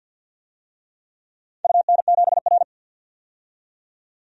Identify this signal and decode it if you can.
Morse code, heard as a keyed tone, about 52 wpm, 700 Hz. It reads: UN7R
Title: DXCC-100 Call Signs - 4 Characters - 50wpm